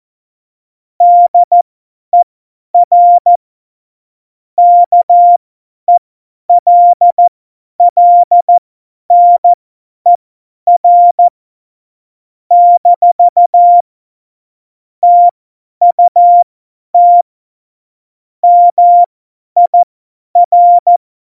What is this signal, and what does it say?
Morse code, heard as a keyed tone, about 14 wpm, spaced as for 10 wpm, 700 Hz. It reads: DER KELLNER - TUT MIR
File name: Morsekurs / Witze / Witz 23. Juli 2019, 50 F70